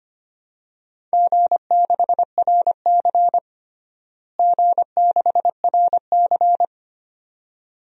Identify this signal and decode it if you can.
Morse code, heard as a keyed tone, about 25 wpm, 700 Hz. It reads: G6RC G6RC
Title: DXCC-100 Call Signs - 4 Characters - Repeated 2x - 25wpm